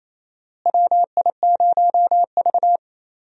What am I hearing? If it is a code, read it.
Morse code, heard as a keyed tone, about 28 wpm, 700 Hz. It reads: WI0V
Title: USA Call Signs - 4 Characters - Repeated 2x - 28wpm